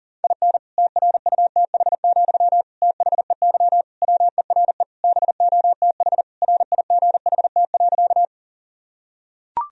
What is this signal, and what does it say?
Morse code, heard as a keyed tone, about 40 wpm, 700 Hz. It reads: IN TRUTH, THEY WERE BOTH RIGHT.
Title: Sentences from Top 1000 Words - Review Part 1 - 40wpm